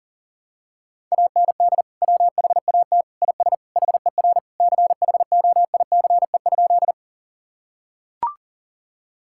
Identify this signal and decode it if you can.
Morse code, heard as a keyed tone, about 37 wpm, 700 Hz. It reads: AND WHAT IS HER CHOICE?
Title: Sentences from Top 1000 Words - Review Part 6 - 40wpm